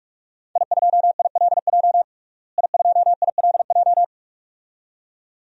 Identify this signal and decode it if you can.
Morse code, heard as a keyed tone, about 44 wpm, 700 Hz. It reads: I2ILJ I2ILJ
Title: DXCC-100 Call Signs - 5 Characters - Repeated 2x - 45wpm